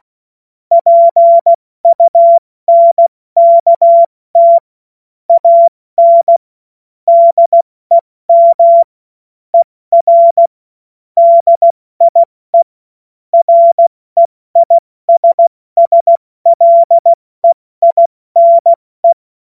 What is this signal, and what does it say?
Morse code, heard as a keyed tone, about 16 wpm, 680 Hz. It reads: PUNKT AN DEM ER DIE REISSLEINE
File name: Morsekurs / Witze / Witz 07. Dezember 2021, 70 F80